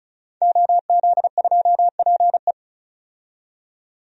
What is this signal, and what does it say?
Morse code, heard as a keyed tone, about 36 wpm, 700 Hz. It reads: OZ2PE
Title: DXCC-100 Call Signs - 5 Characters - 35wpm